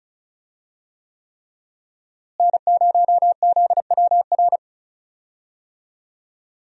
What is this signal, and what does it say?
Morse code, heard as a keyed tone, about 36 wpm, 700 Hz. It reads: N0ZWR
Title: USA Call Signs - 5 Characters - Repeated 3x - 35wpm